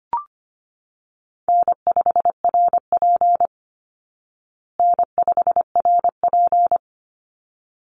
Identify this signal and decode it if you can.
Morse code, heard as a keyed tone, about 25 wpm, 700 Hz. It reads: N5RP N5RP
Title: Easy Call Signs Repeated 2x - 25wpm